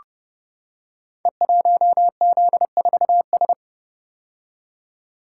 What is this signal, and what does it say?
Morse code, heard as a keyed tone, about 30 wpm, 700 Hz. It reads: E1Z4S
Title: Code Groups - 5 Letter:Numbers - 30wpm